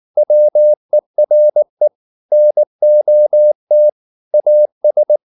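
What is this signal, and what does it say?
Morse code, heard as a keyed tone, about 19 wpm, 600 Hz. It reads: WERE NOT AS
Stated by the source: Morse code at 19 wpm from CW med Gnister, reading Ken Follett's novel Never in English.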